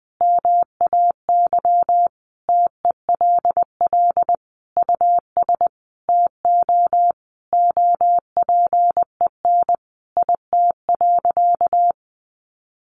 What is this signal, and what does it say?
Morse code, heard as a keyed tone, about 20 wpm, 700 Hz. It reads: MAY TELL US TO OPEN IT.
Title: Sentences from Top 300 Words 20wpm